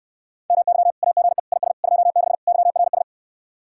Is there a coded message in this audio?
Yes: Morse code from 0.5 to 3.0 s, heard as a keyed tone, about 69 wpm, 700 Hz. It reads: MY AGE IS 26 YRS